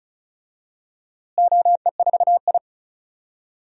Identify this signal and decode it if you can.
Morse code, heard as a keyed tone, about 34 wpm, 700 Hz. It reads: OE4I
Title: DXCC-100 Call Signs - 4 Characters - Repeated 3x - 35wpm